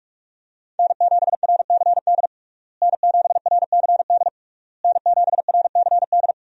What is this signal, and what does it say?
Morse code, heard as a keyed tone, about 41 wpm, 700 Hz. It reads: N7RCD N7RCD N7RCD
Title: USA Call Signs - 5 Characters - Repeated 3x - 45wpm